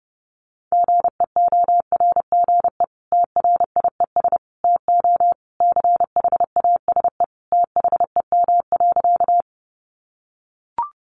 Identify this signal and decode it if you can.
Morse code, heard as a keyed tone, about 30 wpm, 700 Hz. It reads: GEORGE TRIES TO CHASE THEM.
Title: Additional Easy Phrases 30wpm